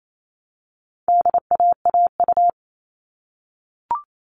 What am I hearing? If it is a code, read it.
Morse code, heard as a keyed tone, about 28 wpm, 700 Hz. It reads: DAAU
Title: All Cut Numbers - 4-Digit 28wpm